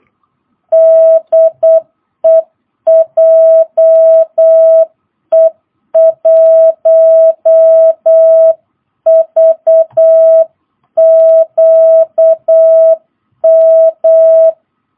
Morse code, heard as a keyed tone, about 8 wpm, 650 Hz. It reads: DE JE1VQM